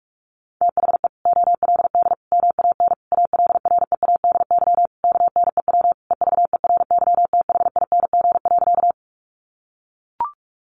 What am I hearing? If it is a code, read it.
Morse code, heard as a keyed tone, about 44 wpm, 700 Hz. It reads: THE OLD MAN ALREADY KNEW EVERYTHING.
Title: Sentences from Top 700 Words 45wpm